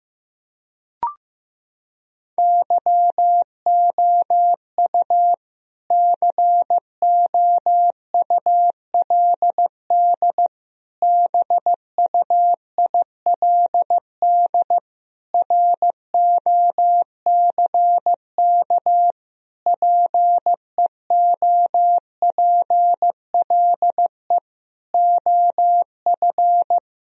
Morse code, heard as a keyed tone, about 15 wpm, 700 Hz. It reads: YOU COULD BUILD ROCK PEOPLE OF
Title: Sentences from Top 1000 Words - Review Part 2 - 15wpm